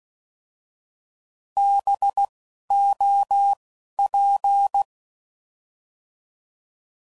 Morse code, heard as a keyed tone, about 16 wpm, 800 Hz. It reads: B O P